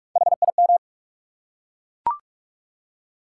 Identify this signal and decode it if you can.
Morse code, heard as a keyed tone, about 43 wpm, 700 Hz. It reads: HIM
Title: Three Letter Words - 45wpm